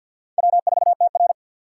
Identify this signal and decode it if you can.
Morse code, heard as a keyed tone, about 46 wpm, 700 Hz. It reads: W4TR
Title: USA Call Signs - 4 Characters - 2x Character Spacing - 50wpm